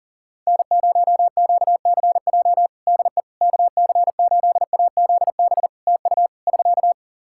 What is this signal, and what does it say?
Morse code, heard as a keyed tone, about 40 wpm, 700 Hz. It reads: N0QCJ DE KC8AZB TU <SK>
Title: Practice QSO - Exchange 4 - Ending - 40wpm